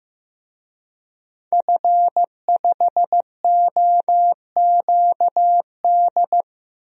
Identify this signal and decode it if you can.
Morse code, heard as a keyed tone, about 15 wpm, 700 Hz. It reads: F5OQD
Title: DXCC-100 Call Signs - 5 Characters - 15wpm